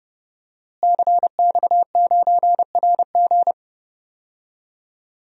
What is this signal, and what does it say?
Morse code, heard as a keyed tone, about 30 wpm, 700 Hz. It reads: CX9RG
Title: DXCC-100 Call Signs - 5 Characters - Repeated 2x - 30wpm